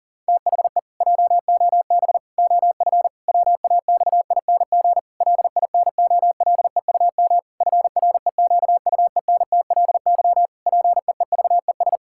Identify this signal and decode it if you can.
Morse code, heard as a keyed tone, about 40 wpm, 700 Hz. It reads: THE JOB OF WAXING LINOLEUM FREQUENTLY PEEVES
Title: Pangram Challenge v2 - 40wpm